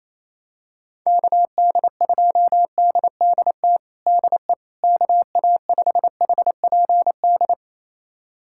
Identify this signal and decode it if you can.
Morse code, heard as a keyed tone, about 28 wpm, 700 Hz. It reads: KD2DDT DE KA5HPD
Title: USA Callsign DE Callsign - 6 Characters - 28wpm